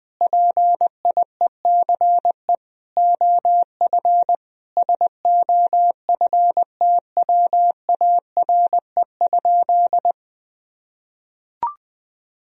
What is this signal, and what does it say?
Morse code, heard as a keyed tone, about 20 wpm, 700 Hz. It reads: PIECE OF SOFTWARE?
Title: Sentences from Top 1000 Words - Review Part 3 - 20wpm